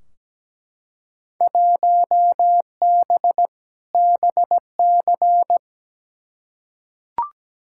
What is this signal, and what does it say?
Morse code, heard as a keyed tone, about 17 wpm, 700 Hz. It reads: 1B BC
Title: Field Day Exchanges - 17wpm